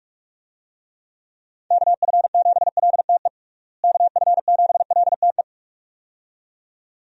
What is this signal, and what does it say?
Morse code, heard as a keyed tone, about 42 wpm, 700 Hz. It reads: KF7LTE KF7LTE
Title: USA Call Signs - 6 Characters - Repeated 2x - 45wpm